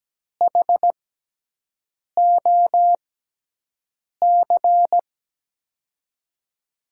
Morse code, heard as a keyed tone, about 17 wpm, spaced as for 6 wpm, 700 Hz. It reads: HOC
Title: USA Call Signs - 5 Characters - 6x Character Spacing - 17wpm